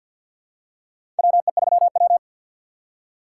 Morse code, heard as a keyed tone, about 48 wpm, 700 Hz. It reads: WE3W
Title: USA Call Signs - 6x Character Spacing - 50wpm